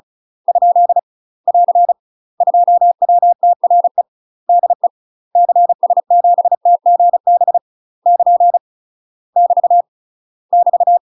Morse code, heard as a keyed tone, about 36 wpm, 700 Hz. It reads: ? <AR> 2WTRE DE CS7TGB <KN> <BT> <BT>
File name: UK MASTER QSOs TRAINING / 35WPM / UK MASTER QSOs 0028